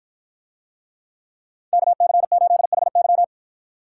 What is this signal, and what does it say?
Morse code, heard as a keyed tone, about 53 wpm, 700 Hz. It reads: KC8HY